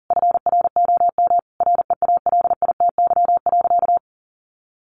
Morse code, heard as a keyed tone, about 40 wpm, 700 Hz. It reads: FROM REALITY.
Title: Sentences from Top 1500 Words 40wpm